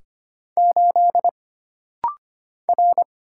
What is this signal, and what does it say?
Morse code, heard as a keyed tone, about 25 wpm, 700 Hz. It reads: 8 R
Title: Single Letter-Number - Rapid-Fire 25wpm